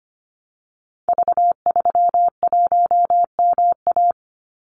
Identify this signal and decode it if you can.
Morse code, heard as a keyed tone, about 25 wpm, 700 Hz. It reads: V31MA
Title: Call Signs - 25wpm